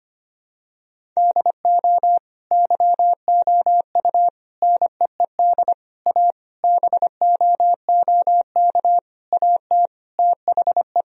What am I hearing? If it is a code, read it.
Morse code, heard as a keyed tone, about 25 wpm, 700 Hz. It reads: DO YOU NEED A BOOK AT THE
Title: Sentences from Top 1000 Words - Review Part 4 - 25wpm